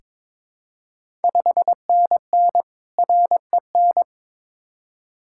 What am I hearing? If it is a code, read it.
Morse code, heard as a keyed tone, about 22 wpm, 700 Hz. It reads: 5NN REN
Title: Ontario QSO Party Exchange 22wpm